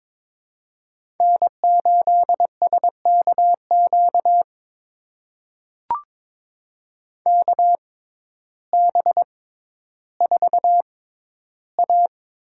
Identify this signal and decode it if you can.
Morse code, heard as a keyed tone, about 22 wpm, spaced as for 12 wpm, 700 Hz. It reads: N8SKQ KB4A
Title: USA Call Signs - 6x Character Spacing - 22wpm